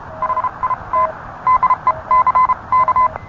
Morse code, heard as a keyed tone, about 30 wpm, spaced as for 26 wpm, 1030 Hz. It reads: HIT DECK